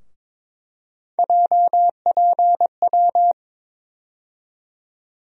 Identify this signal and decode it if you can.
Morse code, heard as a keyed tone, about 22 wpm, 700 Hz. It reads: JPW
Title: USA Call Sign Postfixes - 3 Letters 22wpm